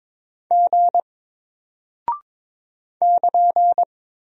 Morse code, heard as a keyed tone, about 22 wpm, 700 Hz. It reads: G <KN>
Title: Single Character (Letter:Number:Punctuation:Prosign) Alt 22wpm